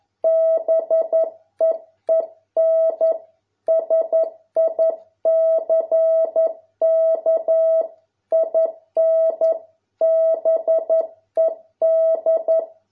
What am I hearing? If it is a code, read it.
Morse code, heard as a keyed tone, about 11 wpm, 650 Hz. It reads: BEEN SICK IN BED